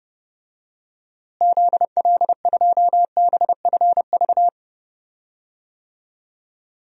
Morse code, heard as a keyed tone, about 30 wpm, 700 Hz. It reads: ZL2BFV
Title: DXCC-100 Call Signs - 6 Characters - Repeated 3x - 30wpm